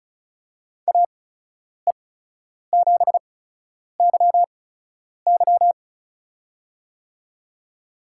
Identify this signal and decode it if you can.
Morse code, heard as a keyed tone, about 36 wpm, spaced as for 10 wpm, 700 Hz. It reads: AE7YY
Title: USA Call Signs - 5 Characters - 8x Character Spacing - 35wpm